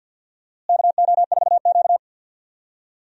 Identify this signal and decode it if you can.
Morse code, heard as a keyed tone, about 48 wpm, 700 Hz. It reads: KO4Q
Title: USA Call Signs - 4 Characters - Repeated 3x - 50wpm